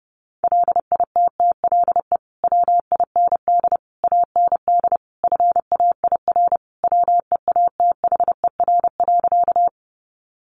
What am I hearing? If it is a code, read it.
Morse code, heard as a keyed tone, about 30 wpm, 700 Hz. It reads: LITTLE WIND AND FAIR WEATHER.
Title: Sentences from Top 1300 Words 30wpm